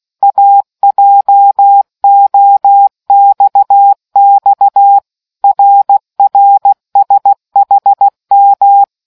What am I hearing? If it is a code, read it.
Morse code, heard as a keyed tone, about 16 wpm, 800 Hz. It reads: AJOXX RRSHM